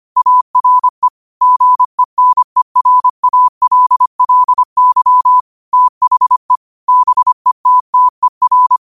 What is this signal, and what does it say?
Morse code, heard as a keyed tone, about 25 wpm, 1000 Hz. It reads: ARE GENERALLY THE BETTER
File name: 25wpm